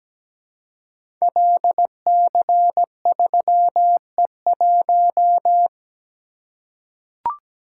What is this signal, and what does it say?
Morse code, heard as a keyed tone, about 17 wpm, 700 Hz. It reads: LC3E1
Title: Code Groups - 5 Letter:Numbers - 17wpm